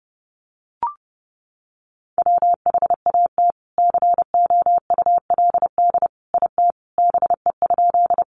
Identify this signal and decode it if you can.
Morse code, heard as a keyed tone, about 30 wpm, 700 Hz. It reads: WHAT COULD IT BE?
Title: Sentences from Top 100 Words 30wpm